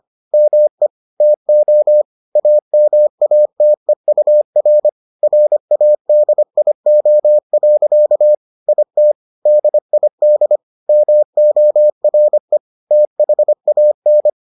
Morse code, heard as a keyed tone, about 25 wpm, 600 Hz. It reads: ME TO AMATEUR RADIO. IT DID MORE THAN